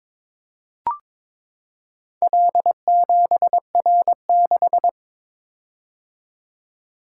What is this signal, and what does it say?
Morse code, heard as a keyed tone, about 22 wpm, 700 Hz. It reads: L7R6
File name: practice / 22wpm / Four Letter-Number Combinations - 22wpm